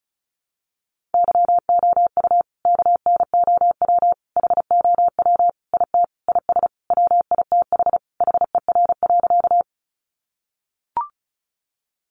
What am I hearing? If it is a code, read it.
Morse code, heard as a keyed tone, about 34 wpm, 700 Hz. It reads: YOU KNOW HOW IT IS WITH HER.
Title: Sentences from Top 100 Words 35wpm